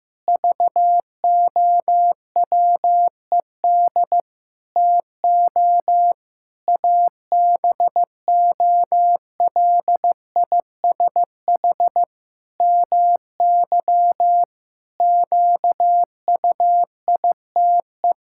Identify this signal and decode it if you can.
Morse code, heard as a keyed tone, about 15 wpm, 700 Hz. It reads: VOWED TO ABOLISH MY QUITE
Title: Pangram Challenge v2 - 15wpm